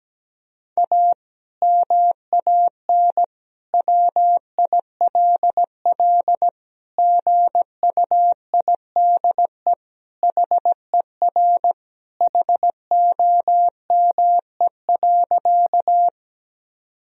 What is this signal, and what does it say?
Morse code, heard as a keyed tone, about 17 wpm, 700 Hz. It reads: A MAN WILL GUIDE HER HOME.
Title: Sentences from Top 1000 Words - Review Part 3 - 17wpm